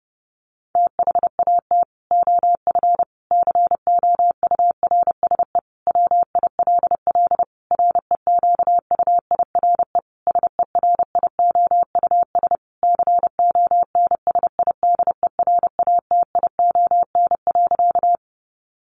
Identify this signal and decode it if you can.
Morse code, heard as a keyed tone, about 30 wpm, 700 Hz. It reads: THAT OF COURSE WILL REQUIRE SERIOUS CONSIDERATION.
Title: Sentences from Top 1500 Words 30wpm